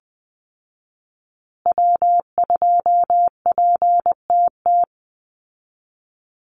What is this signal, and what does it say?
Morse code, heard as a keyed tone, about 19 wpm, 700 Hz. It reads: W2PTT
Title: Call Signs Repeated 3x - 20wpm